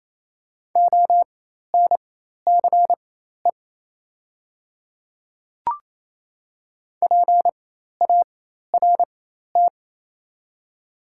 Morse code, heard as a keyed tone, about 27 wpm, spaced as for 13 wpm, 700 Hz. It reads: ONCE PART
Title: 4x Character Spacing - Top 300 Words - 28wpm